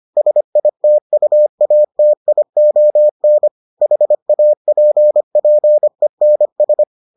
Morse code, heard as a keyed tone, about 25 wpm, 600 Hz. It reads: SITUATION HAPPENS